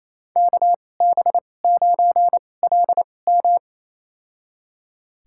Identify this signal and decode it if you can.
Morse code, heard as a keyed tone, about 28 wpm, spaced as for 20 wpm, 700 Hz. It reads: KB9LM
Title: USA Call Signs - 2x Character Spacing - 28wpm